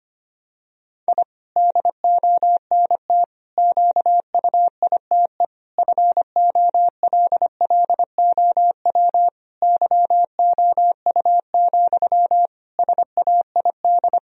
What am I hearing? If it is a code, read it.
Morse code, heard as a keyed tone, about 25 wpm, 700 Hz. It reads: I DONT QUITE FOLLOW YOU, SAID
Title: Sentences from Top 1000 Words - Review Part 4 - 25wpm